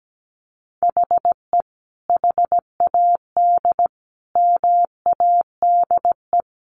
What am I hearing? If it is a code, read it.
Morse code, heard as a keyed tone, about 17 wpm, 700 Hz. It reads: HE HAD MADE
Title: Sentences from Top 500 Words 17wpm